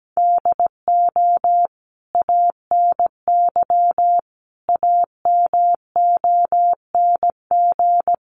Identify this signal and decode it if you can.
Morse code, heard as a keyed tone, about 17 wpm, 700 Hz. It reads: DO ANY AMONG
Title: Sentences from Top 1100 Words 17wpm